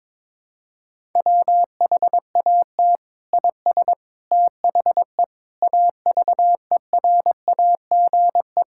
Morse code, heard as a keyed tone, about 22 wpm, 700 Hz. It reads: WHAT IS THE AVERAGE
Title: Sentences from Top 1000 Words - Review Part 6 - 22wpm